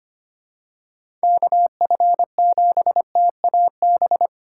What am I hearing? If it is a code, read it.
Morse code, heard as a keyed tone, about 25 wpm, 700 Hz. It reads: KF7TAB
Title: USA Call Signs - 6 Characters - 3x Character Spacing - 25wpm